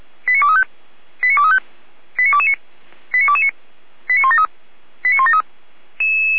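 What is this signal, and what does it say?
Five-tone selective call (ZVEI), heard as quick runs of five tones, tones of about 70 ms. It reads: 89235 89235 89229 89229 89173 89173
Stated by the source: ZVEI five-tone selective call received on SDR.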